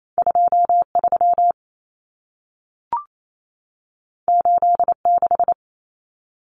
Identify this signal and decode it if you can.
Morse code, heard as a keyed tone, about 28 wpm, 700 Hz. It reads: 23 86
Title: Common Cut Numbers - 2-Digit 28wpm